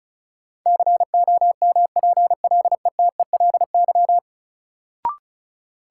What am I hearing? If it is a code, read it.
Morse code, heard as a keyed tone, about 34 wpm, 700 Hz. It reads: COMPLETELY
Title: Sentences from Top 1100 Words - New Word Drill - 35wpm